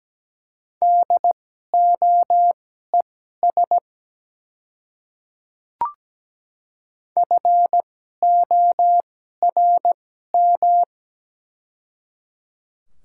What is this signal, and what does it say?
Morse code, heard as a keyed tone, about 17 wpm, spaced as for 12 wpm, 700 Hz. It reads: DOES FORM
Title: 2x Character Spacing - Top 200 Words - 17wpm